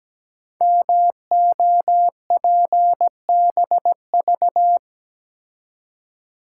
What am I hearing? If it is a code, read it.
Morse code, heard as a keyed tone, about 17 wpm, 700 Hz. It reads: MOPBV
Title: Code Groups - 5 Letters - 17wpm